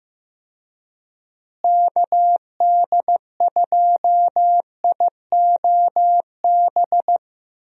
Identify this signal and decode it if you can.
Morse code, heard as a keyed tone, about 15 wpm, 700 Hz. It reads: KD2IOB
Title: USA Call Signs - 6 Characters - Repeated 3x - 15wpm